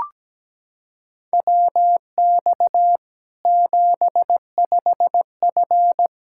Morse code, heard as a keyed tone, about 17 wpm, 700 Hz. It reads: WX 75F
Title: K1USN SST QSO Practice 17wpm